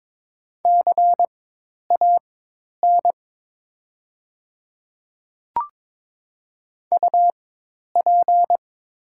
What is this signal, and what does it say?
Morse code, heard as a keyed tone, about 22 wpm, spaced as for 10 wpm, 700 Hz. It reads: CAN UP
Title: 4x Character Spacing - Top 100 Words - 22wpm